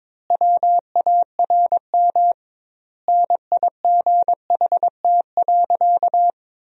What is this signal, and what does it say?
Morse code, heard as a keyed tone, about 22 wpm, 700 Hz. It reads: WARM NIGHT.
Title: 2x Word-Spacing - Sentences from Top 500 Words - 22wpm